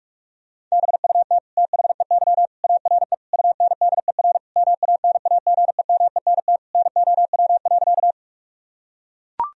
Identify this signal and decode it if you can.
Morse code, heard as a keyed tone, about 43 wpm, 700 Hz. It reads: BUT THEY ARE UNDER MANAGEMENT NOW.
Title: Sentences from Top 1000 Words - Review Part 4 - 45wpm